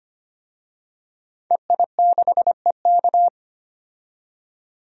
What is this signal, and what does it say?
Morse code, heard as a keyed tone, about 25 wpm, 700 Hz. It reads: EI6EK
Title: DXCC-100 Call Signs - 5 Characters - 25wpm